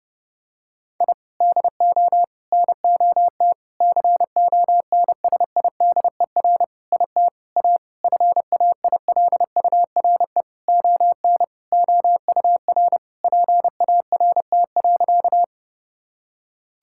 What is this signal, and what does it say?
Morse code, heard as a keyed tone, about 32 wpm, 700 Hz. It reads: I DO NOT CONSIDER IT A FAILURE ON OUR PART.
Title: Sentences from Top 1000 Words - Review Part 6 - 30wpm